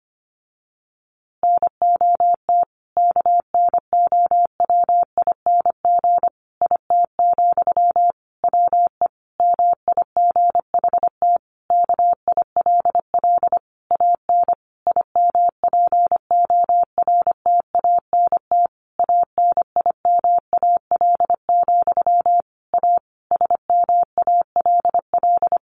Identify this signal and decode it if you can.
Morse code, heard as a keyed tone, about 25 wpm, 700 Hz. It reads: NOT KNOWING IT, WE MIGHT KILL AN IMPORTANT ANIMAL, A SMALL
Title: A Sound Of Thunder 25wpm